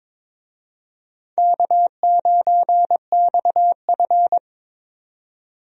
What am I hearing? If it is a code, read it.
Morse code, heard as a keyed tone, about 21 wpm, 700 Hz. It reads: K9XF
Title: USA Call Signs - 4 Characters - 22wpm